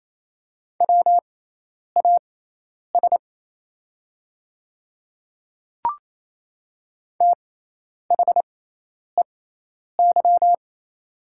Morse code, heard as a keyed tone, about 27 wpm, spaced as for 10 wpm, 690 Hz. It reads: WAS THEY
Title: 6x Character Spacing - Top 100 Words - 28wpm